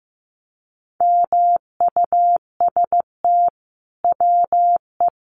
Morse code, heard as a keyed tone, about 15 wpm, 700 Hz. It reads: MUST WE
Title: Sentences from Top 1400 Words 15wpm